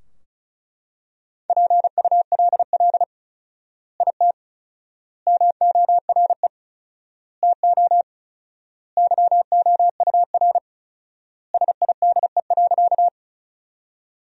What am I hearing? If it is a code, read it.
Morse code, heard as a keyed tone, about 34 wpm, 700 Hz. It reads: PULL IT MORE TO YOUR SIDE.
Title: 4x Word-Spacing - Sentences from Top 400 Words - 35wpm